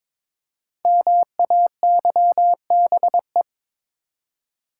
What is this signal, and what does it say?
Morse code, heard as a keyed tone, about 22 wpm, 700 Hz. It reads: MAYBE
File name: 5 Letter Words - First 3 Letters Given - 22wpm